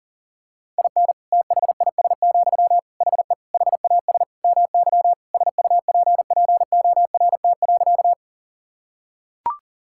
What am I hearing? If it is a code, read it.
Morse code, heard as a keyed tone, about 37 wpm, 700 Hz. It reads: IN THIS, HE HAS MY SUPPORT.
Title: Sentences from Top 1000 Words - Review Part 5 - 40wpm